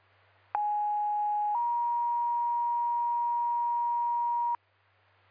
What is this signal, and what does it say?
Two-tone page, heard as a one-second tone, then a three-second tone, about 830 then 950 Hz. It is heard from 0.5 to 4.6 s.